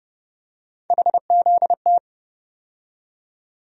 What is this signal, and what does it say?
Morse code, heard as a keyed tone, about 31 wpm, 700 Hz. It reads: HZT
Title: USA Call Sign Postfixes - 3 Letters 30wpm